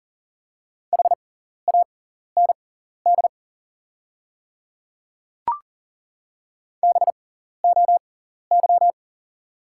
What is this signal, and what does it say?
Morse code, heard as a keyed tone, about 37 wpm, spaced as for 14 wpm, 700 Hz. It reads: HAND BOY
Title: 6x Character Spacing - Top 200 Words - 40wpm